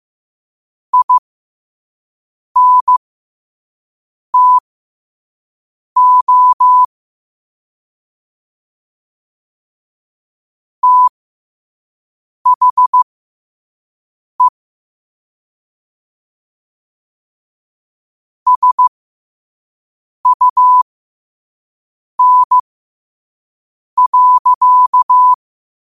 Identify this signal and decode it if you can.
Morse code, heard as a keyed tone, about 15 wpm, spaced as for 5 wpm, 1000 Hz. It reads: INTO THE SUN.
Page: New quotes every day in morse code at 5 Words per minute.